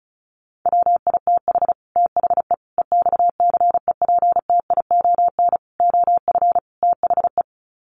Morse code, heard as a keyed tone, about 34 wpm, 700 Hz. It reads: WITH THE EXCEPTION OF THE
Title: Sets of 5 Words Encore 35wpm